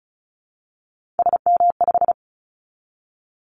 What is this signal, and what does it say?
Morse code, heard as a keyed tone, about 36 wpm, 700 Hz. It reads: SM5
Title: Callsign Prefixes for Top 25 DXCC - 35wpm